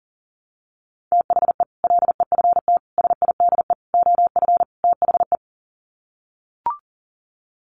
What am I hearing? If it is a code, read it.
Morse code, heard as a keyed tone, about 40 wpm, 700 Hz. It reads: THE LEFT SIDE OF THE
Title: Sets of 5 Words Encore 40wpm